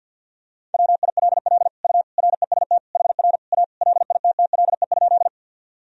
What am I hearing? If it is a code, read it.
Morse code, heard as a keyed tone, about 48 wpm, 700 Hz. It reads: WILL U REST HR A LITTLE?
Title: Sentences from Top 500 Words and Abbreviations 50wpm